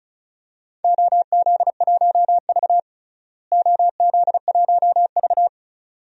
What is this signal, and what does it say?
Morse code, heard as a keyed tone, about 34 wpm, 700 Hz. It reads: OZ1V OZ1V
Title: DXCC-100 Call Signs - 4 Characters - Repeated 2x - 35wpm